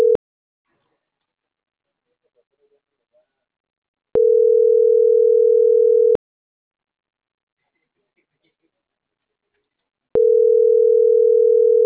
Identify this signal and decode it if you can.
Ringback tone (North America), 440 + 480 Hz.
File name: CENTRAL MAERSK / out alan.vicente